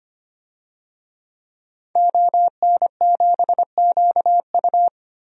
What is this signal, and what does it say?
Morse code, heard as a keyed tone, about 25 wpm, 700 Hz. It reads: ON7QU
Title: DXCC-100 Call Signs - 5 Characters - 25wpm